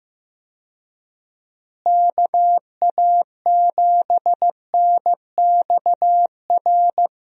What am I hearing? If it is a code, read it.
Morse code, heard as a keyed tone, about 15 wpm, 700 Hz. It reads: KA7NXR
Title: USA Call Signs - 6 Characters - Repeated 2x - 15wpm